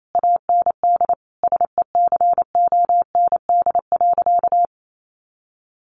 Morse code, heard as a keyed tone, about 28 wpm, 700 Hz. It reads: AND SECOND.
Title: A Sound Of Thunder 28wpm